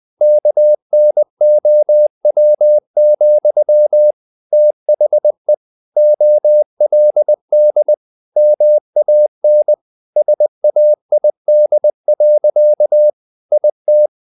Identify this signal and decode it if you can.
Morse code, heard as a keyed tone, about 20 wpm, 600 Hz. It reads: KNOW, THE OLD MAN SAID. IT
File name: the old man and the sea / the old man and the sea part 1 20wpm